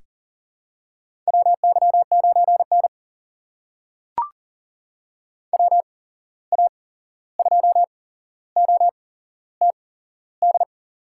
Morse code, heard as a keyed tone, about 40 wpm, spaced as for 15 wpm, 700 Hz. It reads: WY9N WA2OTD
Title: USA Call Signs - 8x Character Spacing - 40wpm